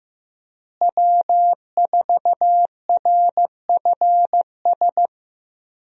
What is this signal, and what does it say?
Morse code, heard as a keyed tone, about 15 wpm, 700 Hz. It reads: W4RFS
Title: USA Call Signs - 5 Characters - Repeated 2x - 15wpm